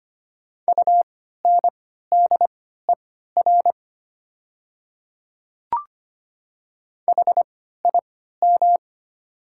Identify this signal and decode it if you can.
Morse code, heard as a keyed tone, about 25 wpm, spaced as for 14 wpm, 700 Hz. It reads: UNDER HIM
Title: 3x Character Spacing - Top 200 Words - 25wpm